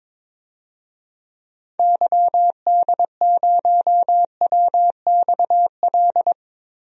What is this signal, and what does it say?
Morse code, heard as a keyed tone, about 22 wpm, 700 Hz. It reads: YD0WXL
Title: DXCC-100 Call Signs - 6 Characters - Repeated 2x - 22wpm